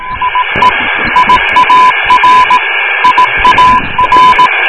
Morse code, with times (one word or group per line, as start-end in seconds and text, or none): 0.2-0.7 IE
1.2-2.6 IAR
3.0-4.5 IAR